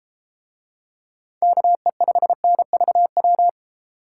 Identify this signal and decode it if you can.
Morse code, heard as a keyed tone, about 32 wpm, 700 Hz. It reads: KE5NVW